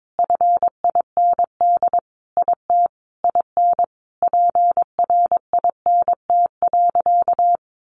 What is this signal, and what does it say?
Morse code, heard as a keyed tone, about 22 wpm, 700 Hz. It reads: FIND IT IN PRINT.
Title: Sentences from Top 1400 Words 22wpm